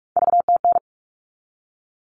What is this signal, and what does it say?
Morse code, heard as a keyed tone, about 44 wpm, 700 Hz. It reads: VTN
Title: All Cut Numbers - 3-Digit 45wpm